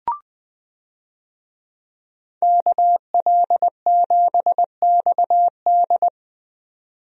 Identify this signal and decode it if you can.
Morse code, heard as a keyed tone, about 20 wpm, 700 Hz. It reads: KL7XD
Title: DXCC-100 Call Signs - 5 Characters - 20wpm